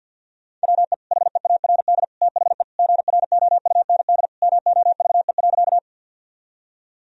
Morse code, heard as a keyed tone, about 50 wpm, 700 Hz. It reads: WE HEARD THE GROUND MOVE.
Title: Sentences from Top 1000 Words - Review Part 2 - 50wpm